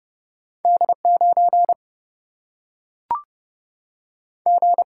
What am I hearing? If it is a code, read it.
Morse code, heard as a keyed tone, about 30 wpm, 700 Hz. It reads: D 9 G
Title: Single-Letter Single-Number 30wpm